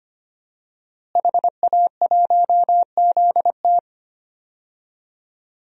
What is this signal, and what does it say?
Morse code, heard as a keyed tone, about 25 wpm, 700 Hz. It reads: HA1ZT
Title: DXCC-100 Call Signs - 5 Characters - Repeated 2x - 25wpm